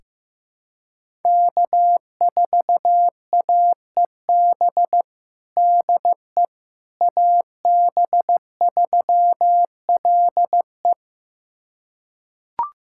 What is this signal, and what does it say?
Morse code, heard as a keyed tone, about 15 wpm, 700 Hz. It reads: K4AEB DE AB3LE
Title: USA Callsign DE Callsign - 5 Characters - 15wpm